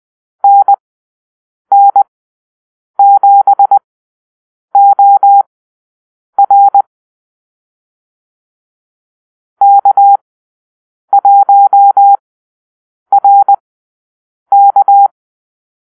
Morse code, heard as a keyed tone, about 20 wpm, spaced as for 7 wpm, 800 Hz. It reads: NN7OR K1RK
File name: arrl / Call-Sign Drill-10